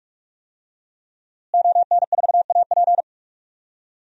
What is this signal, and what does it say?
Morse code, heard as a keyed tone, about 46 wpm, 700 Hz. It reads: ON4AP
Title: DXCC-100 Call Signs - 5 Characters - Repeated 2x - 45wpm